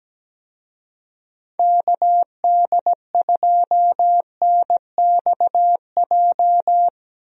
Morse code, heard as a keyed tone, about 17 wpm, 700 Hz. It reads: KD2NXJ
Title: USA Call Signs - 6 Characters - Repeated 2x - 17wpm